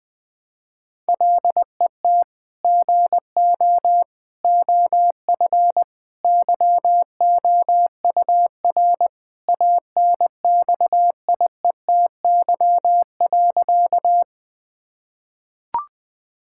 Morse code, heard as a keyed tone, about 20 wpm, 700 Hz. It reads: LET GO OF YOUR ANXIETY.
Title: Sentences from Top 1600 Words 20wpm